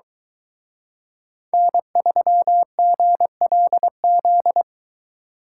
Morse code, heard as a keyed tone, about 23 wpm, 700 Hz. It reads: N3GLZ